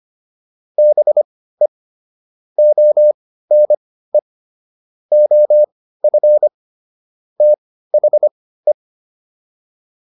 Morse code, heard as a keyed tone, about 25 wpm, spaced as for 15 wpm, 600 Hz. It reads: BE ONE OF THE